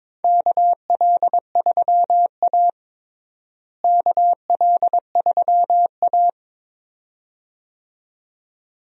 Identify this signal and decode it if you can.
Morse code, heard as a keyed tone, about 22 wpm, 700 Hz. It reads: KL3A KL3A
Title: USA Call Signs - 4 Characters - Repeated 2x - 22wpm